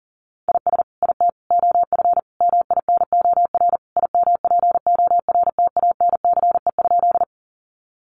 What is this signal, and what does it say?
Morse code, heard as a keyed tone, about 37 wpm, 700 Hz. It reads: IS IT OF MINOR IMPORTANCE?
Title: Sentences from Top 1500 Words 40wpm